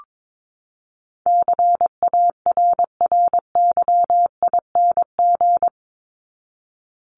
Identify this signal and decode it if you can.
Morse code, heard as a keyed tone, about 22 wpm, 700 Hz. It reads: CARRYING
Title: ING Suffix - Top 100 Words with Base Words 22wpm